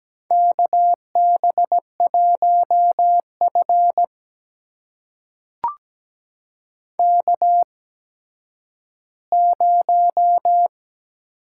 Morse code, heard as a keyed tone, about 17 wpm, spaced as for 10 wpm, 700 Hz. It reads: KB1F K0
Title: USA Call Signs - 4 Characters - 8x Character Spacing - 17wpm